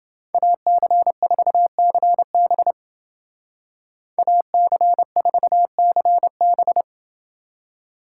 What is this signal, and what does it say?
Morse code, heard as a keyed tone, about 28 wpm, 700 Hz. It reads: AC4CB AC4CB